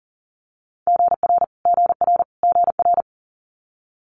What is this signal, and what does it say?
Morse code, heard as a keyed tone, about 40 wpm, 700 Hz. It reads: GR GR GR
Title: Multi-letter Phonemes Repeated 3x 40wpm